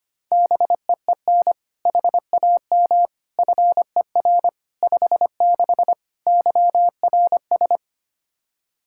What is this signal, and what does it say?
Morse code, heard as a keyed tone, about 25 wpm, 700 Hz. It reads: BEEN HAM FER 56 YRS
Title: Practice QSO - Exchange 3 - More Info - 25wpm